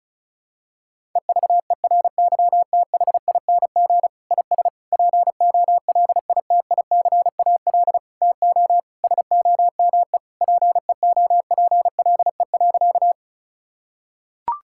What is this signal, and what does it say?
Morse code, heard as a keyed tone, about 34 wpm, 700 Hz. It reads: EVERYTHING IS POLITICAL TO SOME PEOPLE.
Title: Sentences from Top 1000 Words - Review Part 5 - 35wpm